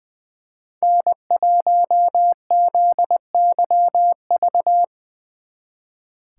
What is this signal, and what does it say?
Morse code, heard as a keyed tone, about 24 wpm, 700 Hz. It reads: N1ZYV
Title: Speed Racing - USA Call Signs 20wpm